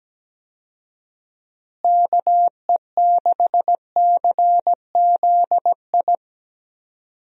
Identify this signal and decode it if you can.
Morse code, heard as a keyed tone, about 17 wpm, 700 Hz. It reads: KE6CZI
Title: USA Call Signs - 6 Characters - Repeated 2x - 17wpm